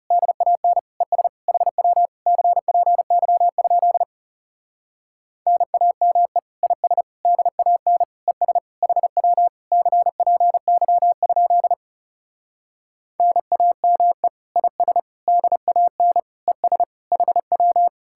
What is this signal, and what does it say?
Morse code, heard as a keyed tone, about 40 wpm, 700 Hz. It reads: DAN ES HW CPY? NAME IS DAN ES HW CPY? NAME IS DAN ES HW